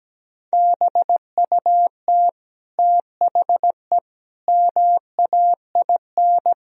Morse code, heard as a keyed tone, about 17 wpm, 700 Hz. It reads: BUT THE MAIN
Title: Sentences from Top 1000 Words - Review Part 2 - 17wpm